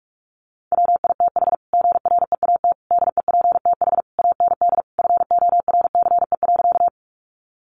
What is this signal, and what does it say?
Morse code, heard as a keyed tone, about 44 wpm, 700 Hz. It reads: WITH GREAT DEPTH AND FORCE.
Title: Sentences from Top 1200 Words 45wpm